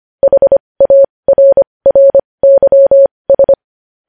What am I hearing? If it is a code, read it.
Morse code, heard as a keyed tone, about 25 wpm, spaced as for 20 wpm, 550 Hz. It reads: HARRYS
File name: podcast / News headlines-1x-25x20